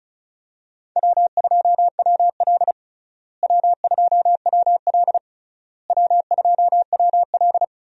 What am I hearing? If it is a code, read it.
Morse code, heard as a keyed tone, about 34 wpm, 700 Hz. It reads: W2WL W2WL W2WL
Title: USA Call Signs - 4 Characters - Repeated 3x - 35wpm